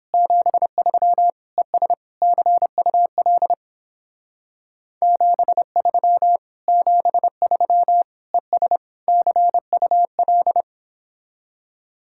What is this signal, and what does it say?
Morse code, heard as a keyed tone, about 29 wpm, 700 Hz. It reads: 73 ES CUL 73 73 ES CUL